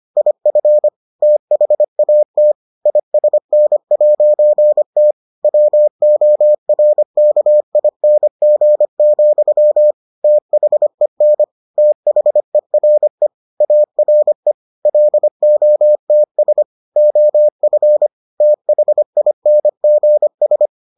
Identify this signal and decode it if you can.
Morse code, heard as a keyed tone, about 25 wpm, 600 Hz. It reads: IF THAT ISN'T WORKING, THEN THERE ARE LOTS OF THINGS